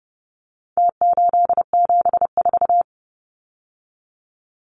Morse code, heard as a keyed tone, about 30 wpm, 700 Hz. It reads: T874
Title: Numbers - Contest Style - 4-Digit 30wpm